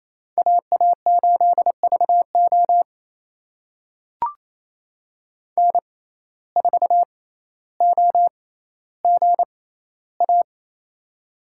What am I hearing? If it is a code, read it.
Morse code, heard as a keyed tone, about 28 wpm, spaced as for 14 wpm, 700 Hz. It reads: AA8VO N4OGA
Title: USA Call Signs - 5 Characters - 6x Character Spacing - 28wpm